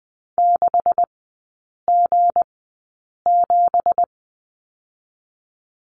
Morse code, heard as a keyed tone, about 20 wpm, 700 Hz. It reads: 6 G 7
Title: 184 Lesson 30e - Letter b - Three Characters 20wpm